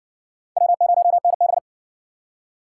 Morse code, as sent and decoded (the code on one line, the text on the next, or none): .-- ----- -. -...
W0NB